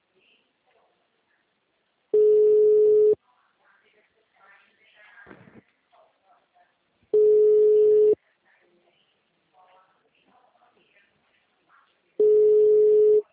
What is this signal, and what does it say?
Ringback tone (Europe, 425 Hz), with speech.
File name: out- -2034- - - .13630